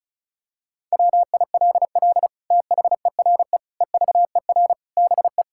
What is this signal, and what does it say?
Morse code, heard as a keyed tone, about 36 wpm, 700 Hz. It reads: WILL THERE EVER BE
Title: Sentences from Top 1000 Words - Review Part 5 - 35wpm